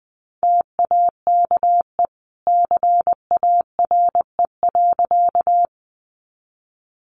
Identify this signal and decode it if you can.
Morse code, heard as a keyed tone, about 19 wpm, 700 Hz. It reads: TAKE CARE.
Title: Sentences from Top 300 Words 20wpm